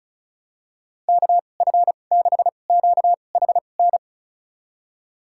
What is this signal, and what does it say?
Morse code, heard as a keyed tone, about 36 wpm, spaced as for 25 wpm, 700 Hz. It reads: KF6QHN
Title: USA Call Signs - 6 Characters - 2x Character Spacing - 35wpm